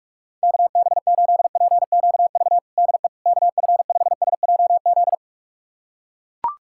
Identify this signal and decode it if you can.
Morse code, heard as a keyed tone, about 41 wpm, 700 Hz. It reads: KB8PQV DE KF5SJZ
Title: USA Callsign DE Callsign - 6 Characters - 45wpm